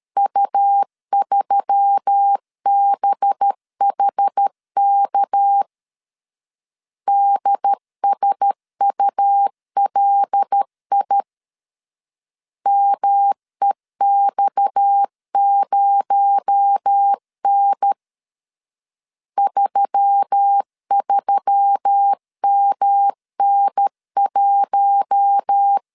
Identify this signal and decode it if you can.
Morse code, heard as a keyed tone, about 13 wpm, 800 Hz. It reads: U3BHK DSULI MEX0N 33MN1
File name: CW / DET / 1 6 1500C S B 12wpm 3431